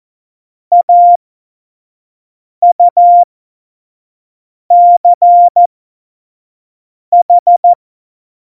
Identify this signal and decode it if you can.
Morse code, heard as a keyed tone, about 14 wpm, spaced as for 5 wpm, 700 Hz. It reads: AUCH
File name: Witz 31. Oktober 17, Tempo 25